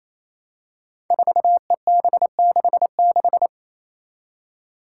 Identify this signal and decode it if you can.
Morse code, heard as a keyed tone, about 28 wpm, 700 Hz. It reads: 4EB66
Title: Five Challenging Letter-Number Combinations 28wpm